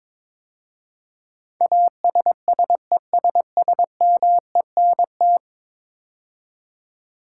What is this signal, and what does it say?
Morse code, heard as a keyed tone, about 22 wpm, 700 Hz. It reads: ASSESSMENT
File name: practice / 22wpm / 10 Letter Words - First 3 Letters Given - 22wpm